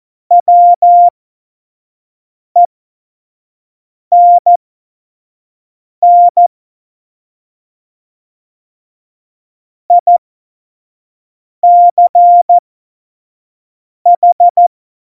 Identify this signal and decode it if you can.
Morse code, heard as a keyed tone, about 14 wpm, spaced as for 5 wpm, 700 Hz. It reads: WENN ICH
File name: Morsekurs / Witze / Witz 22. Dezember 2020, 25 F70